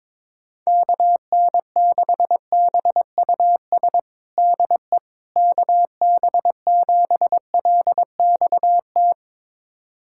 Morse code, heard as a keyed tone, about 22 wpm, 700 Hz. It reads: KN6BUS DE KB7LXT
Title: USA Callsign DE Callsign - 6 Characters - 22wpm